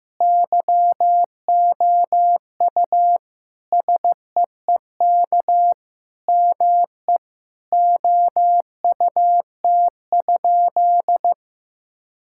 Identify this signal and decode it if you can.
Morse code, heard as a keyed tone, about 16 wpm, 700 Hz. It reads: YOU SEEK ME OUT?
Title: Sentences from Top 1600 Words 15wpm